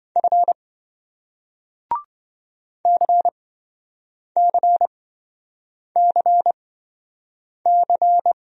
Morse code, heard as a keyed tone, about 28 wpm, spaced as for 7 wpm, 700 Hz. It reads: F CCCC